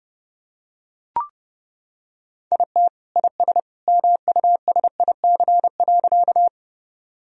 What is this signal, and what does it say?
Morse code, heard as a keyed tone, about 32 wpm, 700 Hz. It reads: IT IS MUSIC.
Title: Sentences from Top 1000 Words - Review Part 1 - 30wpm